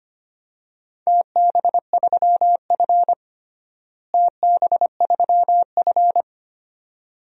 Morse code, heard as a keyed tone, about 25 wpm, 700 Hz. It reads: TB3F TB3F
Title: DXCC-100 Call Signs - 4 Characters - Repeated 2x - 25wpm